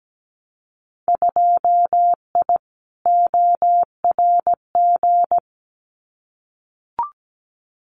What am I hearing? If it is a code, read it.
Morse code, heard as a keyed tone, about 17 wpm, 700 Hz. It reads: 2I ORG
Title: Winter Field Day Exchanges Encore 17wpm